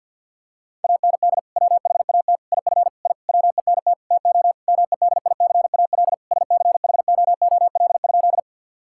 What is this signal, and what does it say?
Morse code, heard as a keyed tone, about 48 wpm, 700 Hz. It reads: AND WHAT IF I WENT TO MEDICAL SCHOOL?
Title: Sentences from Top 1000 Words - Review Part 5 - 50wpm